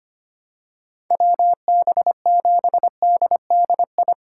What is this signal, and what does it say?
Morse code, heard as a keyed tone, about 25 wpm, 700 Hz. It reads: WB7DDI
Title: USA Call Signs - 6 Characters - 6x Character Spacing - 25wpm